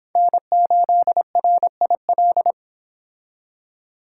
Morse code, heard as a keyed tone, about 25 wpm, 700 Hz. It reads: N8RIL